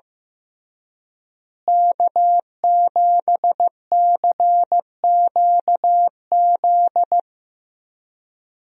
Morse code, heard as a keyed tone, about 15 wpm, 700 Hz. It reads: K7CQZ
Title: USA Call Signs - 5 Characters - Repeated 3x - 15wpm